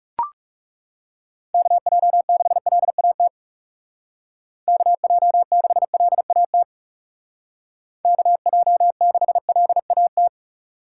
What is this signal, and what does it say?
Morse code, heard as a keyed tone, about 36 wpm, 700 Hz. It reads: KJ6LAT KJ6LAT KJ6LAT
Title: Speed Racing - USA 2x3 Call Signs 30wpm